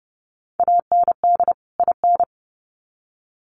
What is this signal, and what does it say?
Morse code, heard as a keyed tone, about 30 wpm, 700 Hz. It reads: AND IN
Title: Sets of 2 Words 30wpm